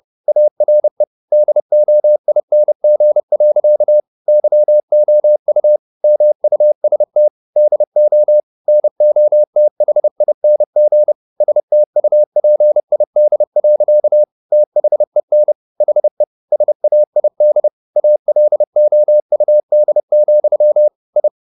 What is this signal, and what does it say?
Morse code, heard as a keyed tone, about 30 wpm, 600 Hz. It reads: ARE DOING. YOU MUST DO NOTHING STUPID. THEN HE SAID ALOUD, I
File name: the old man and the sea / the old man and the sea 1 30wpm